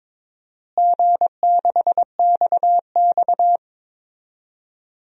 Morse code, heard as a keyed tone, about 22 wpm, 700 Hz. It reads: G6XX
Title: DXCC-100 Call Signs - 4 Characters - Repeated 2x - 22wpm